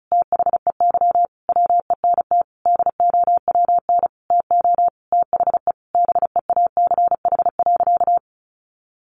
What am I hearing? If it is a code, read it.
Morse code, heard as a keyed tone, about 34 wpm, 700 Hz. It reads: THEY WENT DOWN TO THE BEACH.
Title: Sentences from Top 1500 Words 35wpm